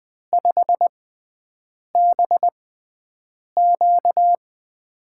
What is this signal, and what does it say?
Morse code, heard as a keyed tone, about 20 wpm, spaced as for 7 wpm, 700 Hz. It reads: 5BQ
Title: USA Call Signs - 5 Characters - 6x Character Spacing - 20wpm